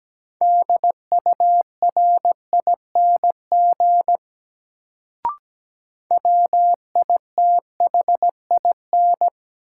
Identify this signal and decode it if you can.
Morse code, heard as a keyed tone, about 17 wpm, 700 Hz. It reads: DURING WITHIN
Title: Sentences from Top 700 Words - New Word Drill - 17wpm